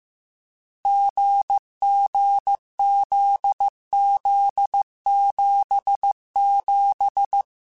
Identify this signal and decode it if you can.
Morse code, heard as a keyed tone, about 15 wpm, 800 Hz. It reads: GGZZ77
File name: Difficult Combos / Difficult Combos-mp3s / GZ GZ7 @13